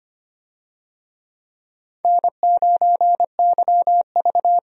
Morse code, heard as a keyed tone, about 25 wpm, 700 Hz. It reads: N9YV
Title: USA Call Signs - 4 Characters - 25wpm